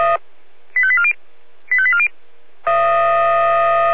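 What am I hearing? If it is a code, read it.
Five-tone selective call (ZVEI), heard as quick runs of five tones, tones of about 70 ms. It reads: 85840 85840
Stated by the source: ZVEI five-tone selective call 85870, received with an SDR.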